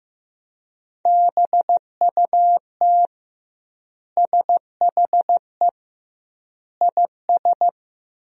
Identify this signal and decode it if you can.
Morse code, heard as a keyed tone, about 15 wpm, 700 Hz. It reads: BUT SHE IS
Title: 2x Word-Spacing - Sentences from Top 500 Words - 15wpm